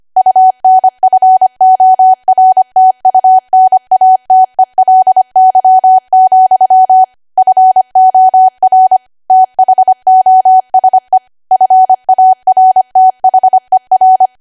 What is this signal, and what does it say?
Morse code, heard as a keyed tone, about 25 wpm, 750 Hz. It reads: UNFORTUNATELY, FOR THOSE FARTHER
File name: cwa / W1AW-Level3 / 25WPM